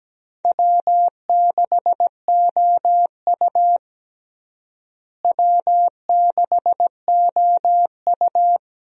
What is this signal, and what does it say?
Morse code, heard as a keyed tone, about 17 wpm, 700 Hz. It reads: W6OU W6OU
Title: USA Call Signs - 4 Characters - Repeated 3x - 17wpm